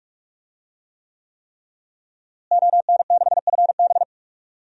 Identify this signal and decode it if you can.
Morse code, heard as a keyed tone, about 41 wpm, 700 Hz. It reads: ON6FB
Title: DXCC-100 Call Signs - 5 Characters - Repeated 3x - 45wpm